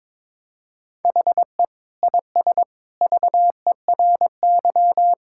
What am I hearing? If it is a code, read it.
Morse code, heard as a keyed tone, about 22 wpm, 700 Hz. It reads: HE IS VERY
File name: practice / 22wpm / Sentences with 2-Word Phrases - 22wpm